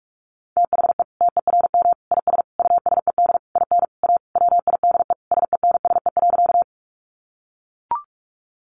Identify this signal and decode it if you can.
Morse code, heard as a keyed tone, about 44 wpm, 700 Hz. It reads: THE TERM IS USED IN A WIDE SENSE.
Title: Sentences from Top 900 Words 45wpm